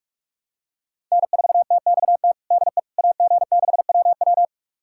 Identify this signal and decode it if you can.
Morse code, heard as a keyed tone, about 45 wpm, 700 Hz. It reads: N4TXT DE AG6WW
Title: USA Callsign DE Callsign - 5 Characters - 45wpm